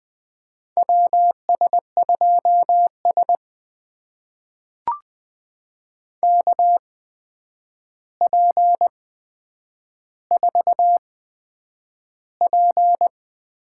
Morse code, heard as a keyed tone, about 20 wpm, spaced as for 9 wpm, 700 Hz. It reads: WS2S KP4P
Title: USA Call Signs - 4 Characters - 8x Character Spacing - 20wpm